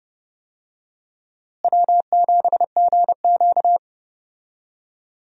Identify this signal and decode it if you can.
Morse code, heard as a keyed tone, about 30 wpm, 700 Hz. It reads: W7GQ
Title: USA Call Signs - 4 Characters - Repeated 3x - 30wpm